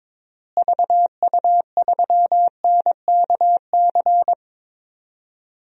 Morse code, heard as a keyed tone, about 22 wpm, 700 Hz. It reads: VU3NKC
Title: DXCC-100 Call Signs - 6 Characters - Repeated 3x - 22wpm